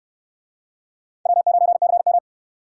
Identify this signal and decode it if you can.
Morse code, heard as a keyed tone, about 68 wpm, 700 Hz. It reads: W9CM